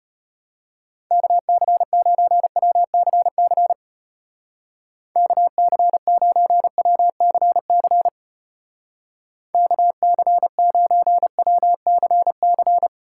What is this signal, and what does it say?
Morse code, heard as a keyed tone, about 31 wpm, 700 Hz. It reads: KC9WCC KC9WCC KC9WCC